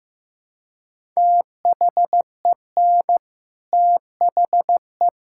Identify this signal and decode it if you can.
Morse code, heard as a keyed tone, about 15 wpm, 700 Hz. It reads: THEN THE
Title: Sentences from Top 1000 Words - Review Part 6 - 15wpm